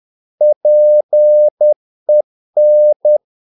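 Morse code, heard as a keyed tone, about 10 wpm, 600 Hz. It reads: PEN
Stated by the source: Morse practice of short Danish words at 10 wpm, from CW med Gnister.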